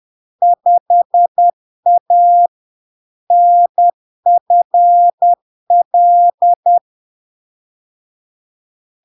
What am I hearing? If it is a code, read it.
Morse code, heard as a keyed tone, about 10 wpm, 700 Hz. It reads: 5A NFL
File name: LARK-FD-2022-Log-in-CW-10wpm-700-Hz